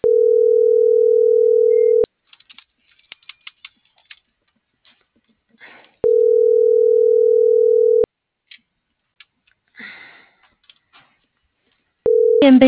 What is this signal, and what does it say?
Ringback tone (North America), 440 + 480 Hz.